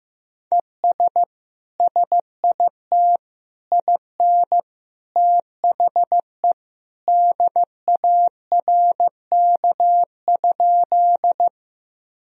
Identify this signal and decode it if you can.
Morse code, heard as a keyed tone, about 15 wpm, 700 Hz. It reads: ES SIT IN THE DARK?
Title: Sentences from Top 500 Words and Abbreviations 15wpm